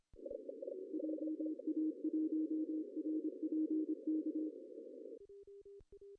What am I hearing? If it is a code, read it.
Morse code, heard as a keyed tone, about 26 wpm, 310 Hz. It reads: JA1RPK
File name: CW / cqww-cw-k5tr